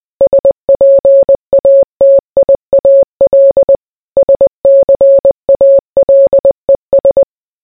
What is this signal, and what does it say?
Morse code, heard as a keyed tone, about 20 wpm, 550 Hz. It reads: SPATIAL SCALES